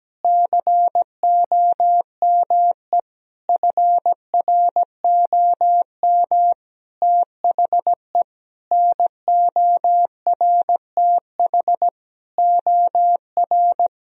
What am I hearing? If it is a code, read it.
Morse code, heard as a keyed tone, about 17 wpm, 700 Hz. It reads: COME FROM THE NORTH OR
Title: Sentences from Top 1000 Words - Review Part 2 - 17wpm